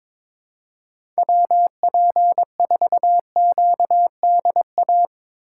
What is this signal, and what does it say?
Morse code, heard as a keyed tone, about 22 wpm, 700 Hz. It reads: WP4QDA
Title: USA Call Signs - 6x Character Spacing - 22wpm